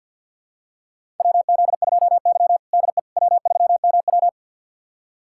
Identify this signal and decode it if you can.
Morse code, heard as a keyed tone, about 52 wpm, 700 Hz. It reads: WZ2Y DE W3MW
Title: USA Callsign DE Callsign - 4 Characters - 50wpm